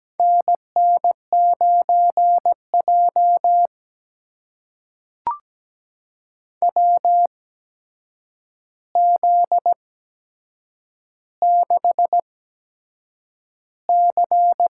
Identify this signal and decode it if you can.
Morse code, heard as a keyed tone, about 17 wpm, spaced as for 7 wpm, 700 Hz. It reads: NN9J WZ6C
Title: USA Call Signs - 4 Characters - 8x Character Spacing - 17wpm